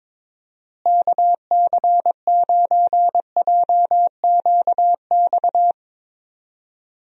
Morse code, heard as a keyed tone, about 22 wpm, 700 Hz. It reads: KC9JQX
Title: USA Call Signs - 6 Characters - Repeated 3x - 22wpm